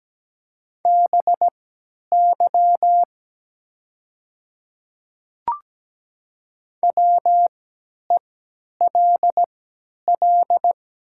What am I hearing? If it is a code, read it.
Morse code, heard as a keyed tone, about 17 wpm, spaced as for 10 wpm, 700 Hz. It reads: BY WELL
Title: 3x Character Spacing - Top 300 Words - 17wpm